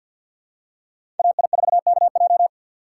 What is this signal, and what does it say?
Morse code, heard as a keyed tone, about 49 wpm, 700 Hz. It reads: AI4KJ
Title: USA Call Signs - 5 Characters - 3x Character Spacing - 50wpm